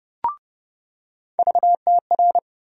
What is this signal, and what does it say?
Morse code, heard as a keyed tone, about 32 wpm, 700 Hz. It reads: VTR
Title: USA Call Sign Postfixes - 3 Letters 30wpm